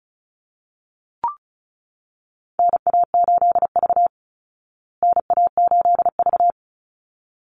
Morse code, heard as a keyed tone, about 34 wpm, 700 Hz. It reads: NA8V NA8V
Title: Call Signs Repeated 3x - 35wpm